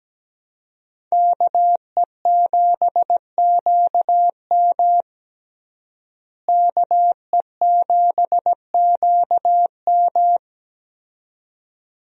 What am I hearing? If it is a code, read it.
Morse code, heard as a keyed tone, about 17 wpm, 700 Hz. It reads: KE7QM KE7QM
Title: USA Call Signs - 5 Characters - Repeated 2x - 17wpm